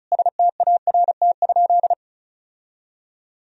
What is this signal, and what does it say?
Morse code, heard as a keyed tone, about 34 wpm, 700 Hz. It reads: START?
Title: Sentences from Top 1000 Words - Review Part 1 - 35wpm